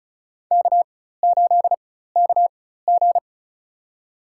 Morse code, heard as a keyed tone, about 36 wpm, spaced as for 16 wpm, 700 Hz. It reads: K8KG
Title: USA Call Signs - 4 Characters - 4x Character Spacing - 35wpm